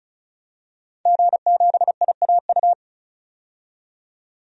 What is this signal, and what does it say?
Morse code, heard as a keyed tone, about 36 wpm, 700 Hz. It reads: G7IAU
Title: DXCC-100 Call Signs - 5 Characters - 35wpm